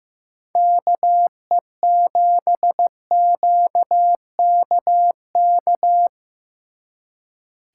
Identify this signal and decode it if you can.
Morse code, heard as a keyed tone, about 18 wpm, 700 Hz. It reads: KE7QKK
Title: Speed Racing - USA 2x3 Call Signs 15wpm